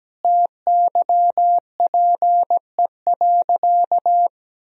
Morse code, heard as a keyed tone, about 17 wpm, 700 Hz. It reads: TYPE.
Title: Sentences from Top 1000 Words - Review Part 3 - 17wpm